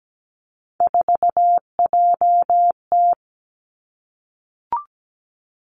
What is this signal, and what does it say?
Morse code, heard as a keyed tone, about 17 wpm, 700 Hz. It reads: 4JT
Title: Three Challenging Letter-Number Combinations 17wpm